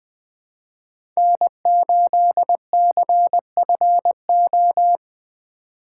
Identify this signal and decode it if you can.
Morse code, heard as a keyed tone, about 20 wpm, 700 Hz. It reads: N8CFO
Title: USA Call Signs - 5 Characters - 4x Character Spacing - 20wpm